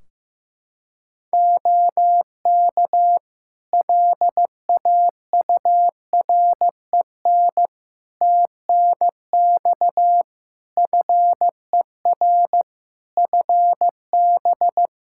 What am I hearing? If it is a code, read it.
Morse code, heard as a keyed tone, about 15 wpm, 700 Hz. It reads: OK LAUREN TNX FER FB
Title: Practice QSO - Exchange 4 - Ending - 15wpm